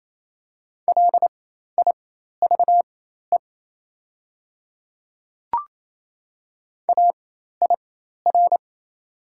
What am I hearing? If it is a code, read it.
Morse code, heard as a keyed tone, about 28 wpm, spaced as for 13 wpm, 690 Hz. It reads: LIVE AIR
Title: 4x Character Spacing - Top 300 Words - 28wpm